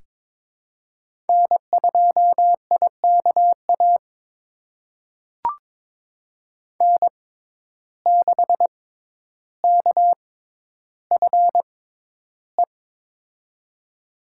Morse code, heard as a keyed tone, about 22 wpm, spaced as for 11 wpm, 700 Hz. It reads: N2IKA N6KFE
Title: USA Call Signs - 5 Characters - 6x Character Spacing - 22wpm